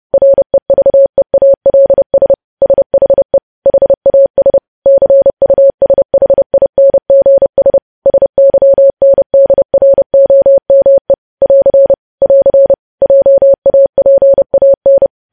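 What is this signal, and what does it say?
Morse code, heard as a keyed tone, about 30 wpm, 550 Hz. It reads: REVEALS SHE HAS CUSHINGS SYNDROME <AR> <AR> JAPAN
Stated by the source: News headlines sent in Morse at 30 words per minute.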